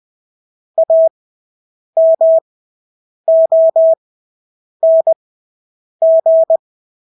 Morse code, heard as a keyed tone, about 20 wpm, spaced as for 8 wpm, 650 Hz. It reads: AMONG
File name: Engelske ord 20/8 wpm